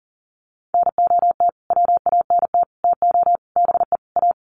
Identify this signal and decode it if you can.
Morse code, heard as a keyed tone, about 40 wpm, 700 Hz. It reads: NOT WANT TO BE A
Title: Sets of 5 Words Encore 40wpm